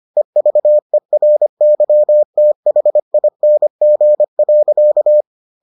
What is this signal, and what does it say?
Morse code, heard as a keyed tone, about 25 wpm, 600 Hz. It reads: EVERYTHING.